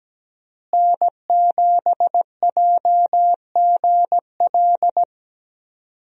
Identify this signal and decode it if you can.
Morse code, heard as a keyed tone, about 17 wpm, 700 Hz. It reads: N7JGL
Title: USA Call Signs - 5 Characters - Repeated 3x - 17wpm